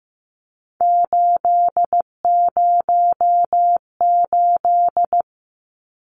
Morse code, heard as a keyed tone, about 15 wpm, 700 Hz. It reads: 808
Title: Numbers - 3-Digit 15wpm